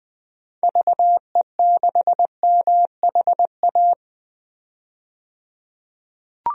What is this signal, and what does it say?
Morse code, heard as a keyed tone, about 20 wpm, 700 Hz. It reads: VE6MHA
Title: DXCC-100 Call Signs - 6 Characters - 20wpm